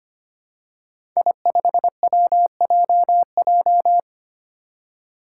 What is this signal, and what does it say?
Morse code, heard as a keyed tone, about 25 wpm, 700 Hz. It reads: I5WJJ
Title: Five Challenging Letter-Number Combinations 25wpm